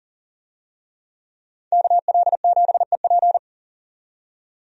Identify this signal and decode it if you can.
Morse code, heard as a keyed tone, about 40 wpm, 700 Hz. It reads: KL7EP
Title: DXCC-100 Call Signs - 5 Characters - 40wpm